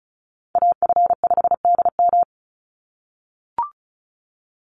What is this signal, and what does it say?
Morse code, heard as a keyed tone, about 34 wpm, 700 Hz. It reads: AF5DM
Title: Call Signs - 35wpm